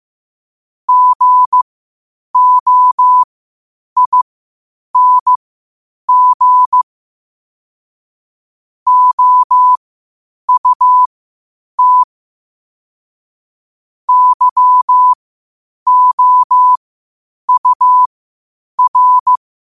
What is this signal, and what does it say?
Morse code, heard as a keyed tone, about 15 wpm, spaced as for 8 wpm, 1000 Hz. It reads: GOING OUT YOUR
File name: 8wpm